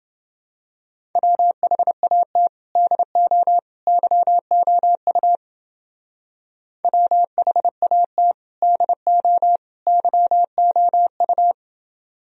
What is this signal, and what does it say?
Morse code, heard as a keyed tone, about 28 wpm, 700 Hz. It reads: WHAT DO YOU WHAT DO YOU